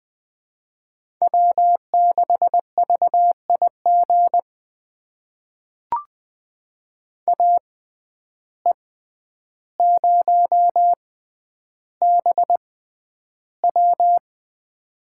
Morse code, heard as a keyed tone, about 20 wpm, spaced as for 10 wpm, 700 Hz. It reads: W6VIG AE0BW
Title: USA Call Signs - 5 Characters - 6x Character Spacing - 20wpm